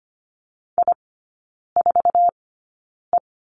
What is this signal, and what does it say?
Morse code, heard as a keyed tone, about 25 wpm, spaced as for 9 wpm, 700 Hz. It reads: I4E
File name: Lesson 14e - Number 2 - Three Characters 25wpm